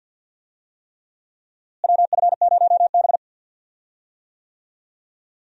Morse code, heard as a keyed tone, about 47 wpm, 700 Hz. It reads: WF0B
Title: USA Call Signs - 4 Characters - Repeated 3x - 50wpm